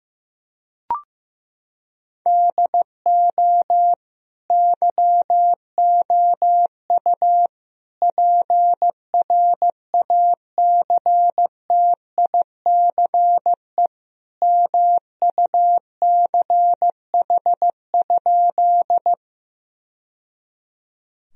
Morse code, heard as a keyed tone, about 14 wpm, 700 Hz. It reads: DO YOU PRACTICE MUCH?
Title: Sentences from Top 1000 Words - Review Part 4 - 15wpm